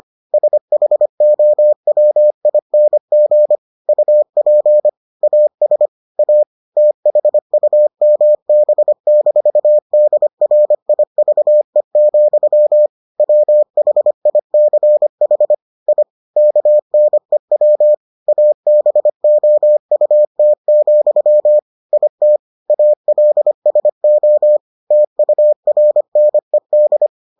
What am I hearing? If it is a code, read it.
Morse code, heard as a keyed tone, about 25 wpm, 600 Hz. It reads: SHOWING UP AS A THUMB-DRIVE, WHICH I KNEW ABOUT, IT ALSO TURNED